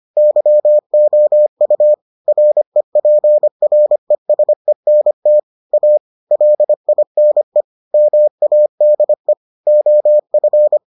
Morse code, heard as a keyed tone, about 25 wpm, 600 Hz. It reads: YOU REPRESENT A LINE MADE OF